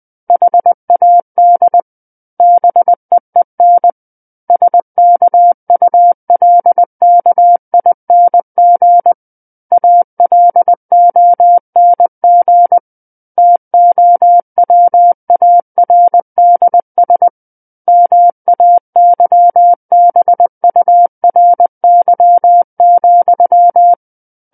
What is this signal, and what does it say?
Morse code, heard as a keyed tone, about 20 wpm, 700 Hz. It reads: HAD BEEN SKULKING ALONG TOWARDS MAYBURY,
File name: 20 WPM